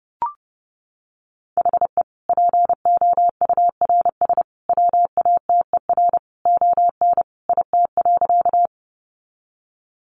Morse code, heard as a keyed tone, about 30 wpm, 700 Hz. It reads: HE POURS WATER ON IT.
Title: Additional Easy Phrases 30wpm